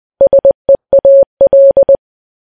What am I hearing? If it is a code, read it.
Morse code, heard as a keyed tone, about 20 wpm, 550 Hz. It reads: SEAL